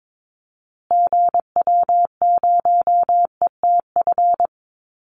Morse code, heard as a keyed tone, about 22 wpm, 700 Hz. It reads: GW0ETF
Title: Call Signs Repeated 2x - 22wpm